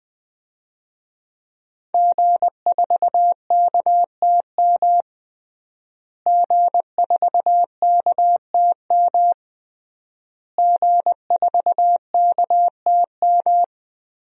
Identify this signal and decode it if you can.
Morse code, heard as a keyed tone, about 20 wpm, 700 Hz. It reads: G4KTM G4KTM G4KTM
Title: DXCC-100 Call Signs - 5 Characters - Repeated 3x - 20wpm